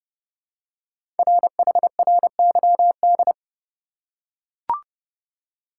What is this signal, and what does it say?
Morse code, heard as a keyed tone, about 30 wpm, 700 Hz. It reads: RHRYD
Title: Code Groups - 5 Letters - 30wpm